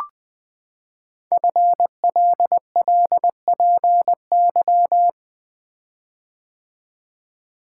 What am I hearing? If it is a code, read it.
Morse code, heard as a keyed tone, about 20 wpm, 700 Hz. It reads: FLLPY
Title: Code Groups - 5 Letters - 20wpm